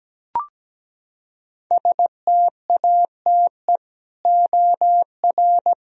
Morse code, heard as a keyed tone, about 17 wpm, 700 Hz. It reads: STATE OR
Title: Sentences from Top 1000 Words - Review Part 3 - 17wpm